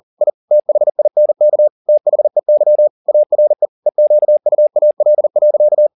Morse code, heard as a keyed tone, about 40 wpm, 600 Hz. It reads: I THINK THEY ARE EQUAL.